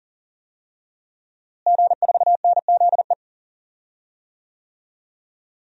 Morse code, heard as a keyed tone, about 40 wpm, 700 Hz. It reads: G4NZE
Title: DXCC-100 Call Signs - 5 Characters - Repeated 2x - 40wpm